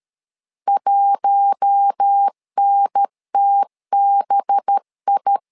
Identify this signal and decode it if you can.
Morse code, heard as a keyed tone, about 13 wpm, 800 Hz. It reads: 1NTBI
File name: LEZIONI / 5 LBVC / 5 LB 12wpm 1906